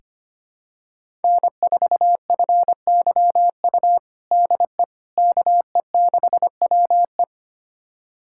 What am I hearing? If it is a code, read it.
Morse code, heard as a keyed tone, about 25 wpm, 700 Hz. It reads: N4FYU DE KE6WE
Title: USA Callsign DE Callsign - 5 Characters - 25wpm